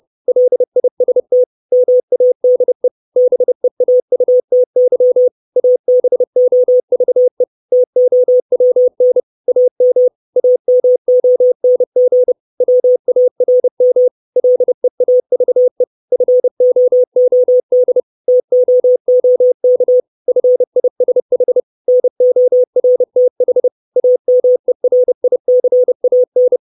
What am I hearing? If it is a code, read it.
Morse code, heard as a keyed tone, about 30 wpm, 500 Hz. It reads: LIST MADE BEAUTY ABOVE TOWN AM AMONG WARM LEAVE FOOD TOOK FISH NORTH AMERICAN
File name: 632 Words 30WPM